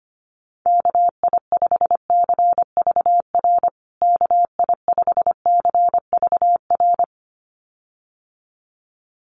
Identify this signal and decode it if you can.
Morse code, heard as a keyed tone, about 25 wpm, 700 Hz. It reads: KI5CVR KI5CVR
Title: USA Call Signs Repeated 2x 25wpm